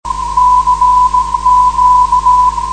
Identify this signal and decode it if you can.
Morse code, heard as a keyed tone, about 16 wpm, 1000 Hz. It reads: CQ
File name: 472khz web swl / 05januar2013 / g3kev 472.6khz